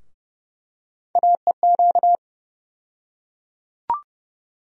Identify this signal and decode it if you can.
Morse code, heard as a keyed tone, about 31 wpm, 700 Hz. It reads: AEQ
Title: USA Call Sign Postfixes - 3 Letters 30wpm